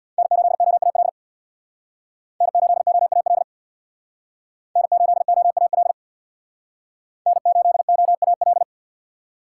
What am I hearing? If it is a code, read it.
Morse code, heard as a keyed tone, about 62 wpm, 700 Hz. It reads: N8OAL N8OAL N8OAL N8OAL
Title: Speed Racing - USA 1x3 Call Signs 50wpm